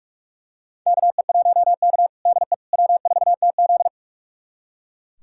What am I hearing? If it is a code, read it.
Morse code, heard as a keyed tone, about 44 wpm, 700 Hz. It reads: KE1K DE W4TZ
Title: USA Callsign DE Callsign - 4 Characters - 45wpm